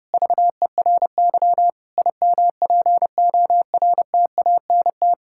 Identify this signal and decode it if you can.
Morse code, heard as a keyed tone, about 30 wpm, 700 Hz. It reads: VERY IMPORTANT
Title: Sentences from Top 1000 Words - Review Part 6 - 30wpm